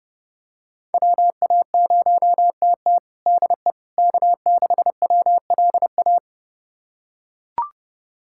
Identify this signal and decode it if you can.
Morse code, heard as a keyed tone, about 30 wpm, 700 Hz. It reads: WA0TT DE K6WLA
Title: USA Callsign DE Callsign - 5 Characters - 30wpm